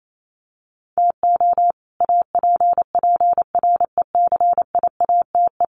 Morse code, heard as a keyed tone, about 28 wpm, 700 Hz. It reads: TO APPRECIATE
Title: Sentences from Top 1300 Words 28wpm